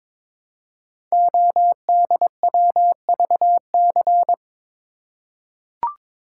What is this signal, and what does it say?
Morse code, heard as a keyed tone, about 22 wpm, 700 Hz. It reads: ODWVC
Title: Code Groups - 5 Letters - 22wpm